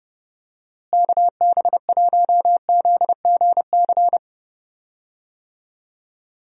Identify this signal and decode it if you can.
Morse code, heard as a keyed tone, about 30 wpm, 700 Hz. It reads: KB1ZGC
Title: USA Call Signs - 6 Characters - Repeated 2x - 30wpm